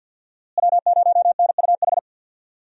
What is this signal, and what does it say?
Morse code, heard as a keyed tone, about 48 wpm, 700 Hz. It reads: W0NUH
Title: USA Call Signs - 5 Characters - Repeated 2x - 50wpm